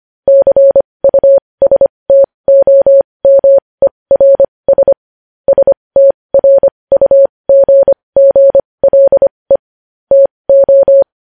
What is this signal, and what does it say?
Morse code, heard as a keyed tone, about 25 wpm, spaced as for 20 wpm, 550 Hz. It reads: CUSTOMERS STRUGGLE TO
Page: News Headlines in Morse Code at 20 WPM